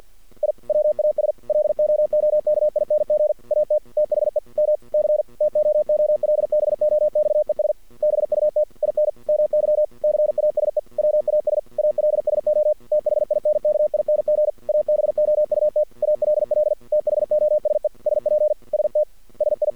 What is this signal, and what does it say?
Morse code, heard as a keyed tone, about 49 wpm, 610 Hz. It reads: I WAS LOOKING AT THE MC TOOLBOXES BUT IN MY CASE WAS ALSO THINKING ABOUT ALL THOSE SPST SI
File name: archived / CRSnet